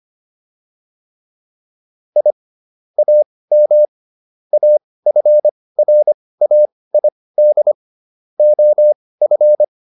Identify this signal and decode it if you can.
Morse code, heard as a keyed tone, about 25 wpm, spaced as for 18 wpm, 600 Hz. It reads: I AM AFRAID OF